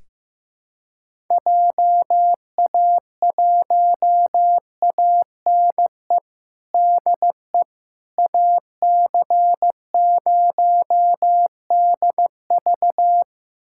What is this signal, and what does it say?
Morse code, heard as a keyed tone, about 15 wpm, 700 Hz. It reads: JA1ANE DE AC0DV
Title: USA Callsign DE Callsign 15wpm